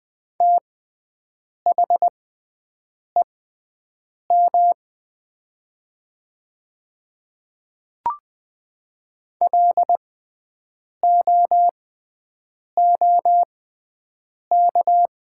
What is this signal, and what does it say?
Morse code, heard as a keyed tone, about 20 wpm, spaced as for 7 wpm, 700 Hz. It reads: THEM LOOK
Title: 6x Character Spacing - Top 200 Words - 20wpm